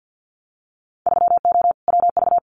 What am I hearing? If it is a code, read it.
Morse code, heard as a keyed tone, about 50 wpm, 700 Hz. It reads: 3O WV
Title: Winter Field Day Exchanges 50wpm